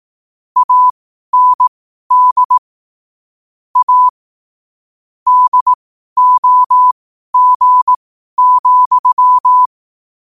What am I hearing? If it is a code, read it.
Morse code, heard as a keyed tone, about 18 wpm, spaced as for 12 wpm, 1000 Hz. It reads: AND A DOG,
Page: New quotes every day in morse code at 12 Words per minute.